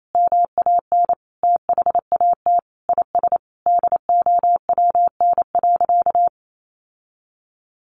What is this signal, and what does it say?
Morse code, heard as a keyed tone, about 28 wpm, 700 Hz. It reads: MAN THAT IS DOWN.
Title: Sentences from Top 1300 Words 28wpm